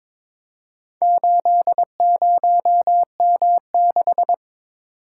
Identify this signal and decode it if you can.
Morse code, heard as a keyed tone, about 22 wpm, 700 Hz. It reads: 80M6
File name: practice / 22wpm / Four Letter-Number Combinations - 22wpm